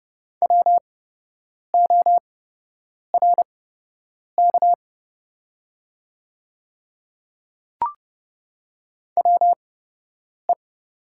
Morse code, heard as a keyed tone, about 30 wpm, spaced as for 8 wpm, 700 Hz. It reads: WORK WE
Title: 8x Character Spacing - Top 300 Words - 30wpm